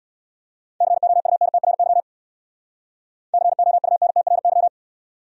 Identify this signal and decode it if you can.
Morse code, heard as a keyed tone, about 64 wpm, 700 Hz. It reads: COUNTRY COUNTRY